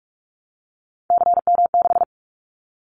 Morse code, heard as a keyed tone, about 46 wpm, 700 Hz. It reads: CM6
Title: Callsign Prefixes for Top 100 DXCC - 45wpm